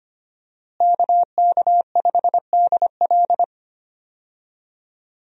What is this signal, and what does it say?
Morse code, heard as a keyed tone, about 25 wpm, 700 Hz. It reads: KK5DL
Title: USA Call Signs - 5 Characters - Repeated 3x - 25wpm